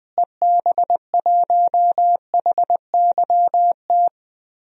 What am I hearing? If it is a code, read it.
Morse code, heard as a keyed tone, about 20 wpm, 700 Hz. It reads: EB1HYT
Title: DXCC-100 Call Signs - 6 Characters - Repeated 3x - 20wpm